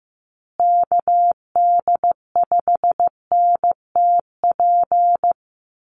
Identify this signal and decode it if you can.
Morse code, heard as a keyed tone, about 15 wpm, 700 Hz. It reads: KD5NTP
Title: USA Call Signs Repeated 2x 15wpm